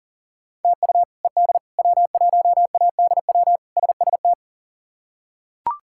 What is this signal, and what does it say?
Morse code, heard as a keyed tone, about 40 wpm, 700 Hz. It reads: TU ED W1ADW SST
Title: K1USN SST QSO Practice - 40wpm